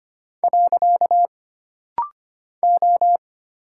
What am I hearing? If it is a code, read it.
Morse code, heard as a keyed tone, about 25 wpm, 700 Hz. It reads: . O
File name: Single Character (Letter:Number:Punctuation:Prosign) - Rapid-Fire - 25wpm